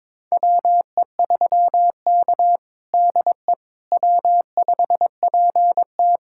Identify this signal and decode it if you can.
Morse code, heard as a keyed tone, about 22 wpm, 700 Hz. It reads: WE3K DE W5PT
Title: USA Callsign DE Callsign - 4 Characters - 22wpm